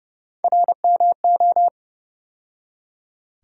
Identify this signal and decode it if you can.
Morse code, heard as a keyed tone, about 30 wpm, 700 Hz. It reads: RMO
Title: USA Call Sign Postfixes - 3 Letters 30wpm